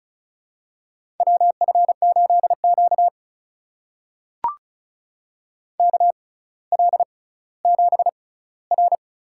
Morse code, heard as a keyed tone, about 36 wpm, spaced as for 18 wpm, 700 Hz. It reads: WF8Q KL7R
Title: USA Call Signs - 4 Characters - 6x Character Spacing - 35wpm